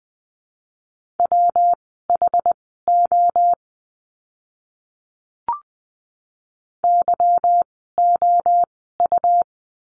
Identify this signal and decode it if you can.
Morse code, heard as a keyed tone, about 20 wpm, spaced as for 14 wpm, 700 Hz. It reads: WHO YOU
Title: 237 Lesson 38f - Letter q - Words 20wpm